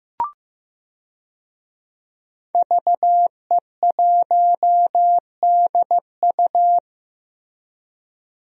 Morse code, heard as a keyed tone, about 15 wpm, 700 Hz. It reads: VE1DU
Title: DXCC-100 Call Signs - 5 Characters - Repeated 2x - 15wpm